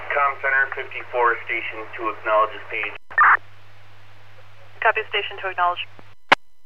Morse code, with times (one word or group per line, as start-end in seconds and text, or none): none